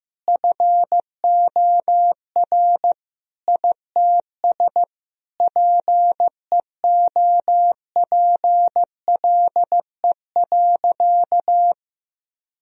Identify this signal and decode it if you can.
Morse code, heard as a keyed tone, about 15 wpm, 700 Hz. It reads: FOR ITS PEOPLE.
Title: Sentences from Top 1000 Words - Review Part 3 - 15wpm